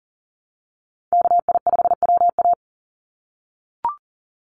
Morse code, heard as a keyed tone, about 40 wpm, 700 Hz. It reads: KI5WA
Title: NA SOTA Chaser Callsigns 40wpm